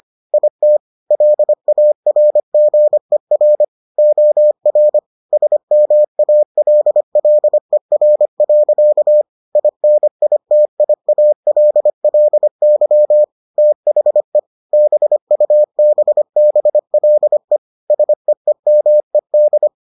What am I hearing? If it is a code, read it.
Morse code, heard as a keyed tone, about 25 wpm, 600 Hz. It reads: IT LARGER OR SMALLER. INITIALLY THE BUBBLE SEEMED